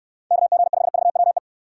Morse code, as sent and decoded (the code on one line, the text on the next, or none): -.- --. ..... ...- .-- .
KG5VWE